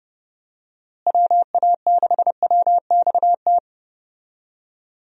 Morse code, heard as a keyed tone, about 30 wpm, 700 Hz. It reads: WA6WXT
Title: USA Call Signs - 6 Characters - Repeated 2x - 30wpm